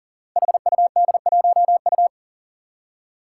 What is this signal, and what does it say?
Morse code, heard as a keyed tone, about 40 wpm, 700 Hz. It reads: HUD1U
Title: Five Challenging Letter-Number Combinations 40wpm